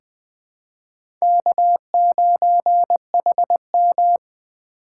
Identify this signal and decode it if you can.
Morse code, heard as a keyed tone, about 20 wpm, 700 Hz. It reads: K9HM
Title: USA Call Signs - 4 Characters - 2x Character Spacing - 20wpm